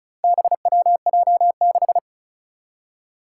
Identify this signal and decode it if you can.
Morse code, heard as a keyed tone, about 36 wpm, 700 Hz. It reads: BWJ6
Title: Four Challenging Letter-Number Combinations 35wpm